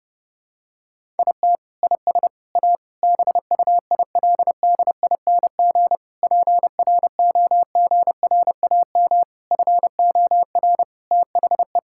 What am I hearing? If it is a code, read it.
Morse code, heard as a keyed tone, about 30 wpm, 700 Hz. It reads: IT IS A BUILDING PROGRAM FOR THE
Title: Sentences from Top 1000 Words - Review Part 5 - 30wpm